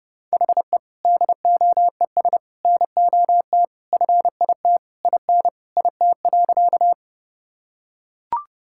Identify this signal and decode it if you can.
Morse code, heard as a keyed tone, about 32 wpm, 700 Hz. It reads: HE DOES NOT FIT IN IT.
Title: Sentences from Top 1000 Words - Review Part 5 - 30wpm